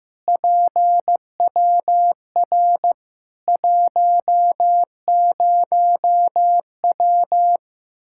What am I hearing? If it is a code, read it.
Morse code, heard as a keyed tone, about 15 wpm, 700 Hz. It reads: PWR 10W
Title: Practice QSO - Exchange 2 - Info - 15wpm